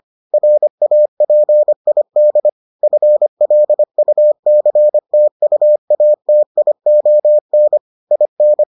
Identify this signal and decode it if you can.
Morse code, heard as a keyed tone, about 25 wpm, 600 Hz. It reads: RAPID FLUCTUATION IN